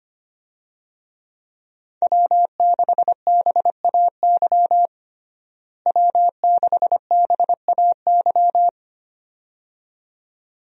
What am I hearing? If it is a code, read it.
Morse code, heard as a keyed tone, about 25 wpm, 700 Hz. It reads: W6BAY W6BAY
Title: USA Call Signs - 5 Characters - Repeated 2x - 25wpm